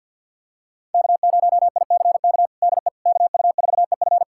Morse code, heard as a keyed tone, about 51 wpm, 700 Hz. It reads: K0ICK DE KU4EF
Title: USA Callsign DE Callsign - 5 Characters - 50wpm